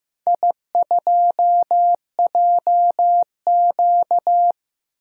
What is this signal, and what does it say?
Morse code, heard as a keyed tone, about 15 wpm, 700 Hz. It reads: I2JQ
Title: DXCC-100 Call Signs - 4 Characters - 15wpm